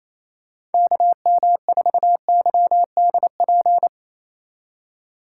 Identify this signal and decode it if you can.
Morse code, heard as a keyed tone, about 28 wpm, 700 Hz. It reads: KM4YDP
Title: USA Call Signs - 6 Characters - 3x Character Spacing - 28wpm